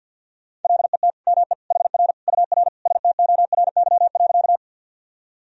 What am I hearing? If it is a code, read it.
Morse code, heard as a keyed tone, about 46 wpm, 700 Hz. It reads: LET ME HR UR STORY.
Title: Sentences from Top 300 Words and Abbreviations 50wpm